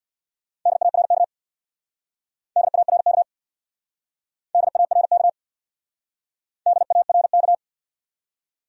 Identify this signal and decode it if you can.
Morse code, heard as a keyed tone, about 59 wpm, 700 Hz. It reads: DARK DARK DARK DARK
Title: Speed-Racing Top 500 Words 50wpm